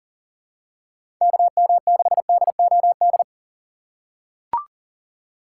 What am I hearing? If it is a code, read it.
Morse code, heard as a keyed tone, about 40 wpm, 700 Hz. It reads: KM6DOD
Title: USA Call Signs - 6 Characters - 3x Character Spacing - 40wpm